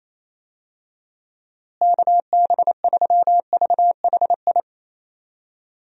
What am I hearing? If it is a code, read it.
Morse code, heard as a keyed tone, about 28 wpm, 700 Hz. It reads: KB3VHI
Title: USA Call Signs - 6 Characters - Repeated 2x - 28wpm